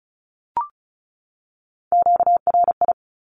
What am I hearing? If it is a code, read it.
Morse code, heard as a keyed tone, about 34 wpm, 700 Hz. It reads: QRI
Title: Q-Codes 35wpm